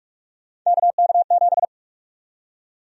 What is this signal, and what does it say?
Morse code, heard as a keyed tone, about 43 wpm, 700 Hz. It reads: KK7
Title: USA Call Sign Prefixes - Repeated 2x - 45wpm